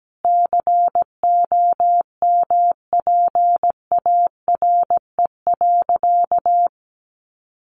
Morse code, heard as a keyed tone, about 17 wpm, 700 Hz. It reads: COMPARE.
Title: Sentences from Top 1300 Words 17wpm